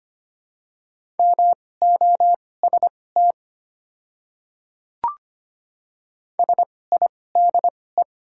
Morse code, heard as a keyed tone, about 25 wpm, spaced as for 18 wpm, 700 Hz. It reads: MOST SIDE
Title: 2x Character Spacing - Top 300 Words - 25wpm